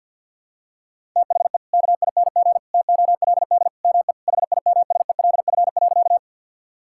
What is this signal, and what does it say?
Morse code, heard as a keyed tone, about 50 wpm, 700 Hz. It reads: THE KING TOLD ME HIMSELF.
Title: Sentences from Top 1600 Words 50wpm